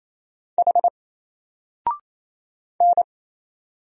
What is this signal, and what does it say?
Morse code, heard as a keyed tone, about 27 wpm, 700 Hz. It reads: H N
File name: Single-Letter-Number-28wpm